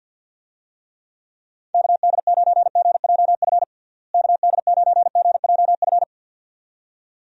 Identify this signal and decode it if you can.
Morse code, heard as a keyed tone, about 51 wpm, 700 Hz. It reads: KD9GJF KD9GJF
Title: USA Call Signs - 6 Characters - Repeated 2x - 50wpm